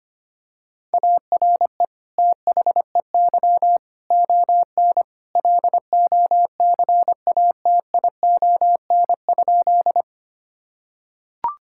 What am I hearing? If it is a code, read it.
Morse code, heard as a keyed tone, about 25 wpm, 700 Hz. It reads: ARE THEY ON LOCATION?
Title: Sentences from Top 1000 Words - Review Part 6 - 25wpm